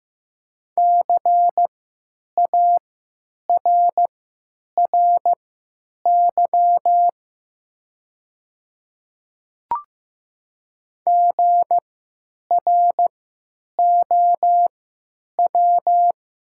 Morse code, heard as a keyed tone, about 15 wpm, spaced as for 9 wpm, 700 Hz. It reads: CARRY GROW
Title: 3x Character Spacing - Top 300 Words - 15wpm